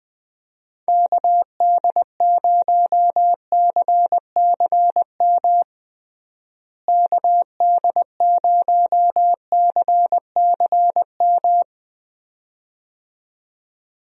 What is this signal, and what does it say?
Morse code, heard as a keyed tone, about 20 wpm, 700 Hz. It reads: KD0CCM KD0CCM
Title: USA Call Signs - 6 Characters - Repeated 2x - 20wpm